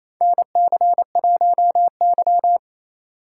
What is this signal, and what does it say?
Morse code, heard as a keyed tone, about 28 wpm, 700 Hz. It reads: NC1Y
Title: USA Call Signs - 4 Characters - 4x Character Spacing - 28wpm